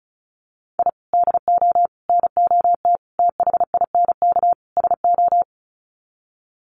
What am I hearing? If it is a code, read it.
Morse code, heard as a keyed tone, about 34 wpm, 700 Hz. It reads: I DO NOT THINK SO
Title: Sets of 5 Words 35wpm